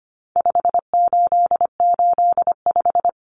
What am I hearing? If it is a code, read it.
Morse code, heard as a keyed tone, about 25 wpm, 700 Hz. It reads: 5885
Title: Numbers - 4-Digit 25wpm